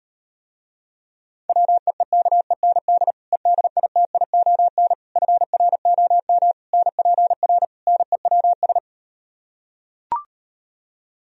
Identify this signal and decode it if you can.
Morse code, heard as a keyed tone, about 38 wpm, 700 Hz. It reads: WEEKEND EDITION FROM NPR NEWS